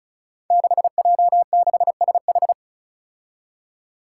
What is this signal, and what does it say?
Morse code, heard as a keyed tone, about 36 wpm, 700 Hz. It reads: 6J6SH
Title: Five Challenging Letter-Number Combinations 35wpm